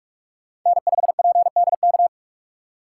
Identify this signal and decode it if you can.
Morse code, heard as a keyed tone, about 41 wpm, 690 Hz. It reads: N5PDK
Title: USA Call Signs - 3x Character Spacing - 45wpm